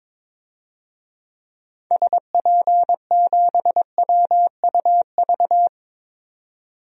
Morse code, heard as a keyed tone, about 22 wpm, 700 Hz. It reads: SP7WUV
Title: DXCC-100 Call Signs - 6 Characters - 22wpm